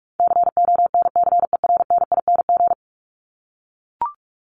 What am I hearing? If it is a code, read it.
Morse code, heard as a keyed tone, about 44 wpm, 700 Hz. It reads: CONCERNING
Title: ING Suffix - Top 100 Words with Base Words 45wpm